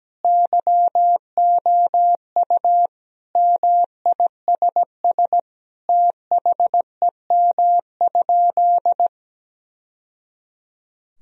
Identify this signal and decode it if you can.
Morse code, heard as a keyed tone, about 17 wpm, 700 Hz. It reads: YOU MISS THEM?
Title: Sentences from Top 1000 Words - Review Part 3 - 17wpm